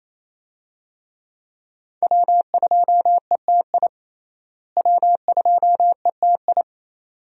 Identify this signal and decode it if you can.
Morse code, heard as a keyed tone, about 28 wpm, 700 Hz. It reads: W2ETI W2ETI
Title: USA Call Signs - 5 Characters - Repeated 2x - 28wpm